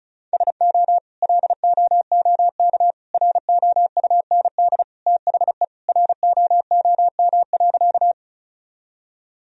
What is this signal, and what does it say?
Morse code, heard as a keyed tone, about 36 wpm, 700 Hz. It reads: SO LOOK ROUND THE ROOM.
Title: Sentences from Top 1000 Words - Review Part 1 - 35wpm